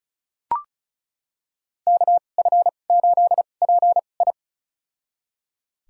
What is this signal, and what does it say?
Morse code, heard as a keyed tone, about 36 wpm, spaced as for 25 wpm, 700 Hz. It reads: KF8PI
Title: USA Call Signs - 5 Characters - 2x Character Spacing - 35wpm